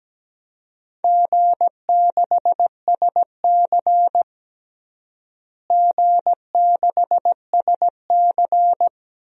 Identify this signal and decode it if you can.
Morse code, heard as a keyed tone, about 17 wpm, 700 Hz. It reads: G6SC G6SC
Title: DXCC-100 Call Signs - 4 Characters - Repeated 3x - 17wpm